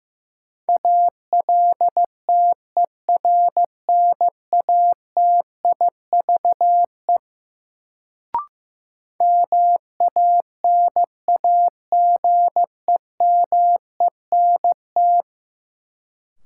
Morse code, heard as a keyed tone, about 15 wpm, 700 Hz. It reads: ALTERNATIVE MANAGEMENT
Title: Sentences from Top 1000 Words - Review Part 4 - New Word Drill - 15wpm